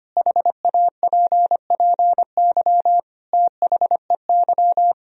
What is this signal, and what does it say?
Morse code, heard as a keyed tone, about 25 wpm, 700 Hz. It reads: HAPPY THEY
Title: Sentences from Top 1000 Words - Review Part 3 - 25wpm